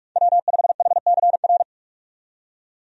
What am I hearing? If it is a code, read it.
Morse code, heard as a keyed tone, about 46 wpm, 700 Hz. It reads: W5HCR
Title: USA Call Signs - 5 Characters - 45wpm